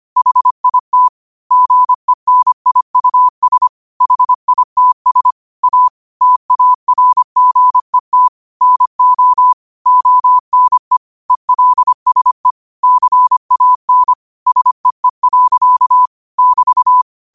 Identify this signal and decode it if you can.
Morse code, heard as a keyed tone, about 25 wpm, 1000 Hz. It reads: HIT GENIUS HITS A TARGET NO ONE ELSE CAN SEE. <BT>